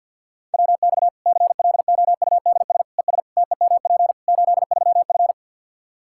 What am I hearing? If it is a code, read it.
Morse code, heard as a keyed tone, about 50 wpm, 700 Hz. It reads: WX CLOUDS ES TEMP 83F
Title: Practice QSO - Exchange 2 - Info - 50wpm